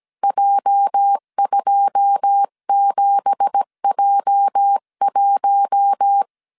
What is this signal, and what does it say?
Morse code, heard as a keyed tone, about 17 wpm, 800 Hz. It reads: J27J1
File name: CW / DET / 1 9 1500C S B 16wpm 2856